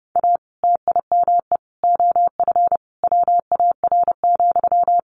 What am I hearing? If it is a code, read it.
Morse code, heard as a keyed tone, about 30 wpm, 700 Hz. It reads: A TIME OF WAR,
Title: Sentences from Top 1500 Words 30wpm